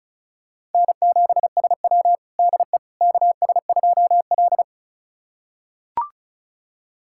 Morse code, heard as a keyed tone, about 34 wpm, 700 Hz. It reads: N7SW DE KS2L
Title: USA Callsign DE Callsign - 4 Characters - 35wpm